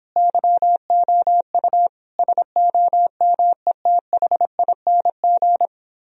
Morse code, heard as a keyed tone, about 29 wpm, 700 Hz. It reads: YOU SOMETHING